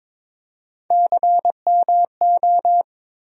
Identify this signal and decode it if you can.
Morse code, heard as a keyed tone, about 21 wpm, 700 Hz. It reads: CMO
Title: USA Call Sign Postfixes - 3 Letters 22wpm